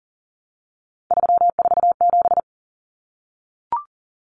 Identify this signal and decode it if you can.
Morse code, heard as a keyed tone, about 40 wpm, 700 Hz. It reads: 347
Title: Numbers - 3-Digit 40wpm